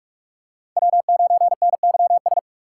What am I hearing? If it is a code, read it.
Morse code, heard as a keyed tone, about 44 wpm, 700 Hz. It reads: W9NYS